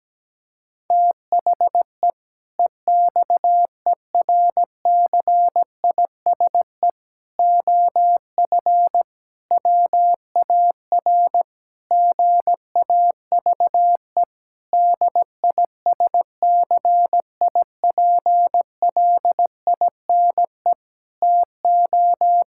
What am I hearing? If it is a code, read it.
Morse code, heard as a keyed tone, about 17 wpm, 700 Hz. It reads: THE EXERCISE OF WAR GAVE DISCIPLINE TO
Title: Sentences from Top 1000 Words - Review Part 6 - 17wpm